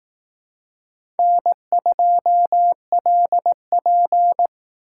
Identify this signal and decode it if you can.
Morse code, heard as a keyed tone, about 18 wpm, 700 Hz. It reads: N2LP
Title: Speed Racing - USA Call Signs 15wpm